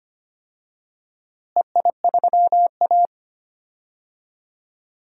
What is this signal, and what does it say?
Morse code, heard as a keyed tone, about 25 wpm, 700 Hz. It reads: EI3A
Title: DXCC-100 Call Signs - 4 Characters - Repeated 3x - 25wpm